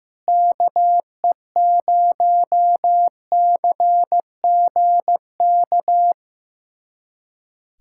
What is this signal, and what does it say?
Morse code, heard as a keyed tone, about 15 wpm, 700 Hz. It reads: KE0CGK
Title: USA Callsign DE Callsign 15wpm